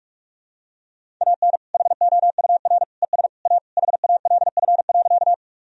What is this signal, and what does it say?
Morse code, heard as a keyed tone, about 41 wpm, 700 Hz. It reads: AN HOUR ES A HALF.
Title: Sentences from Top 400 Words and Abbreviations 45wpm